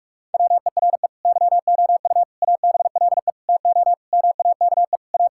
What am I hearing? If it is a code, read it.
Morse code, heard as a keyed tone, about 41 wpm, 700 Hz. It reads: WERE YOU ABLE TO MAKE A
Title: Sentences from Top 1000 Words - Review Part 6 - 45wpm